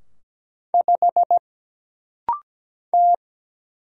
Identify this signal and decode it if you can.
Morse code, heard as a keyed tone, about 17 wpm, 700 Hz. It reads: 5 T
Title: Single Letter-Number Alt - Rapid-Fire 17wpm